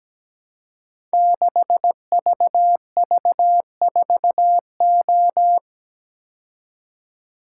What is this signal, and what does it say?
Morse code, heard as a keyed tone, about 17 wpm, 700 Hz. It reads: 6VV4O
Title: Code Groups - 5 Letter:Numbers - 17wpm